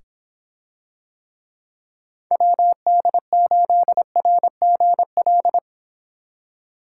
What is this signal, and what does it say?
Morse code, heard as a keyed tone, about 25 wpm, 700 Hz. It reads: WD8RGL